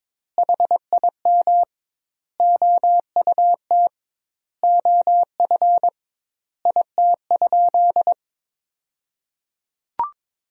Morse code, heard as a keyed tone, about 22 wpm, 700 Hz. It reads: HIM OUT OF IT?
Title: 2x Word-Spacing - Sentences from Top 300 Words - 22wpm